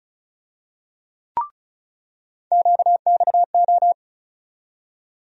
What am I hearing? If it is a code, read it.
Morse code, heard as a keyed tone, about 36 wpm, 700 Hz. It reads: QXO
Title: USA Call Sign Postfixes - 3 Letters 35wpm